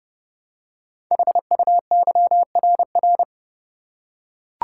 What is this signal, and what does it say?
Morse code, heard as a keyed tone, about 28 wpm, 690 Hz. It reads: HUYRR
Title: Code Groups - 5 Letters - 30wpm